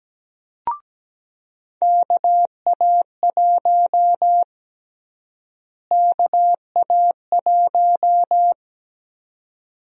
Morse code, heard as a keyed tone, about 17 wpm, 700 Hz. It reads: KA1 KA1
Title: USA Call Sign Prefixes - Repeated 2x - 17wpm